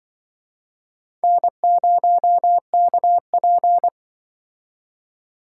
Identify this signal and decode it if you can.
Morse code, heard as a keyed tone, about 26 wpm, 700 Hz. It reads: N0KP